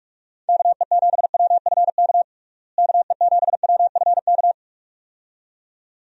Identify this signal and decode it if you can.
Morse code, heard as a keyed tone, about 43 wpm, 700 Hz. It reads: KE7WFK KE7WFK
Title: USA Call Signs - 6 Characters - Repeated 2x - 45wpm